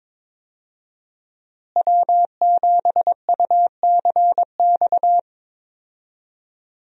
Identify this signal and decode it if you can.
Morse code, heard as a keyed tone, about 22 wpm, 700 Hz. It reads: W7UCX
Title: USA Call Signs - 5 Characters - Repeated 3x - 22wpm